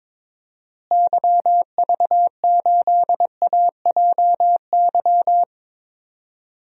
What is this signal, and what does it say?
Morse code, heard as a keyed tone, about 22 wpm, 700 Hz. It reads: YV8AJY
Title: DXCC-100 Call Signs - 6 Characters - Repeated 3x - 22wpm